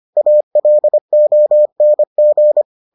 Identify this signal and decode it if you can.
Morse code, heard as a keyed tone, about 25 wpm, 600 Hz. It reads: ALONG